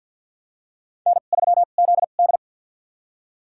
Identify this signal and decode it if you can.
Morse code, heard as a keyed tone, about 53 wpm, spaced as for 37 wpm, 700 Hz. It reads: N3ZD
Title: USA Call Signs - 4 Characters - 2x Character Spacing - 50wpm